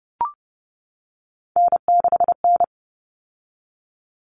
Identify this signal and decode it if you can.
Morse code, heard as a keyed tone, about 30 wpm, 700 Hz. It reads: N6N
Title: Common Cut Numbers - 3-Digit 30wpm